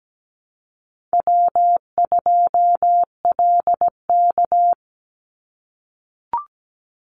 Morse code, heard as a keyed tone, about 17 wpm, 700 Hz. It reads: W2LK
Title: Easy Call Signs - 17wpm